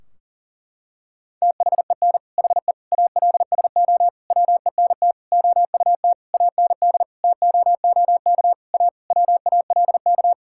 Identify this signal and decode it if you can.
Morse code, heard as a keyed tone, about 40 wpm, 700 Hz. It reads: THEN HE ALSO WENT OUT AND TOOK A WALK
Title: Sentences from Top 1600 Words 40wpm